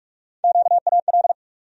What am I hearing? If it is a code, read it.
Morse code, heard as a keyed tone, about 46 wpm, 700 Hz. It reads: QAL
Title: USA Call Sign Postfixes - 3 Letters 45wpm